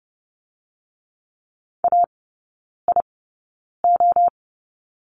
Morse code, heard as a keyed tone, about 32 wpm, spaced as for 9 wpm, 700 Hz. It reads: AIO
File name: Lesson 18e - Letter m - Three Characters 30wpm